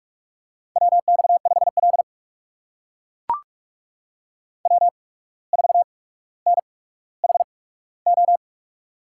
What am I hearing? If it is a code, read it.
Morse code, heard as a keyed tone, about 44 wpm, spaced as for 18 wpm, 700 Hz. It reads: WX5L W4NHO
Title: USA Call Signs - 8x Character Spacing - 45wpm